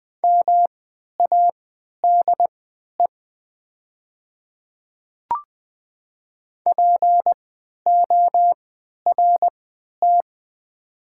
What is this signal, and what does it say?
Morse code, heard as a keyed tone, about 20 wpm, spaced as for 11 wpm, 690 Hz. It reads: MADE PORT
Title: 3x Character Spacing - Top 500 Words - 20wpm